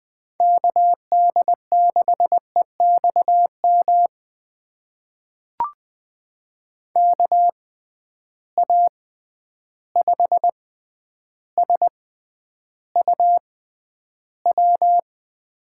Morse code, heard as a keyed tone, about 20 wpm, spaced as for 10 wpm, 700 Hz. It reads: KD6EXM KA5SUW
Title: USA Call Signs - 6 Characters - 6x Character Spacing - 20wpm